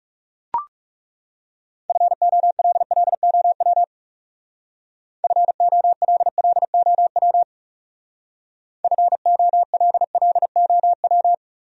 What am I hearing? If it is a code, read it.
Morse code, heard as a keyed tone, about 36 wpm, 700 Hz. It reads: FOLLOW FOLLOW FOLLOW
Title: Speed-Racing Top 500 Words 30wpm